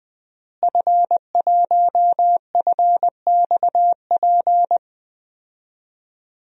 Morse code, heard as a keyed tone, about 20 wpm, 700 Hz. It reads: F1FXP
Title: DXCC-100 Call Signs - 5 Characters - Repeated 3x - 20wpm